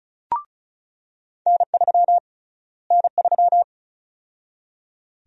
Morse code, heard as a keyed tone, about 34 wpm, 700 Hz. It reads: N3 N3
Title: USA Call Sign Prefixes - Repeated 2x - 35wpm